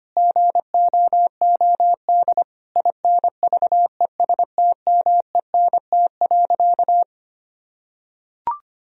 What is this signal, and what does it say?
Morse code, heard as a keyed tone, about 25 wpm, 700 Hz. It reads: GOOD INVESTMENT.
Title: Sentences from Top 1000 Words - Review Part 4 - 25wpm